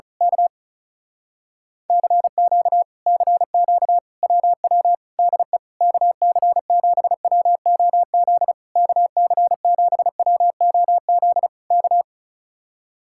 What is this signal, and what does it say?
Morse code, heard as a keyed tone, about 32 wpm, 700 Hz. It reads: K CQ CQ WW DE KC7WOZ KC7WOZ K
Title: Speed-Racing Common QSO Phrases Encore - 35wpm